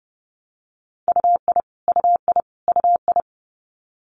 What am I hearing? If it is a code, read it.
Morse code, heard as a keyed tone, about 31 wpm, 700 Hz. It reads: UI UI UI
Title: Multi-letter Phonemes Repeated 3x 30wpm